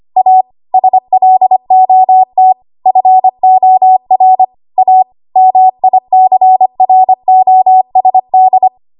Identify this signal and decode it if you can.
Morse code, heard as a keyed tone, about 25 wpm, 750 Hz. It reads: A SLOT FOR A MICROSD
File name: Archive / 25 WPM / 25WPM